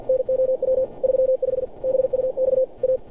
Morse code, heard as a keyed tone, about 47 wpm, 520 Hz. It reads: NOW 35 BUX A
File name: archived / CRSnet